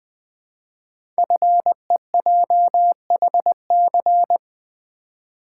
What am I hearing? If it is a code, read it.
Morse code, heard as a keyed tone, about 20 wpm, 700 Hz. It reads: FEJHC
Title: Code Groups - 5 Letters - 20wpm